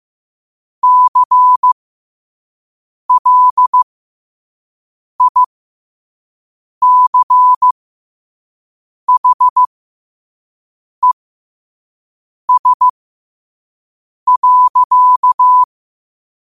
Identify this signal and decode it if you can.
Morse code, heard as a keyed tone, about 15 wpm, spaced as for 5 wpm, 1000 Hz. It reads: CLICHES.
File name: qotdfiles / 5wpm